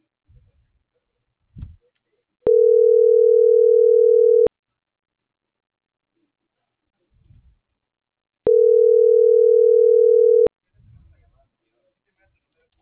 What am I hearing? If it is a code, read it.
Ringback tone (North America), 440 + 480 Hz.